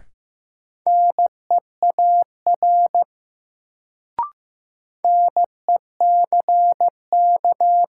Morse code, heard as a keyed tone, about 15 wpm, 700 Hz. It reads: NEAR NECK
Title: Sets of 5 Words Encore - New Word Drill - 15wpm